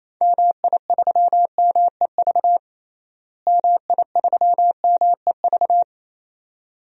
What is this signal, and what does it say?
Morse code, heard as a keyed tone, about 27 wpm, 700 Hz. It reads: MI3MEV MI3MEV
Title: DXCC-100 Call Signs - 6 Characters - Repeated 2x - 28wpm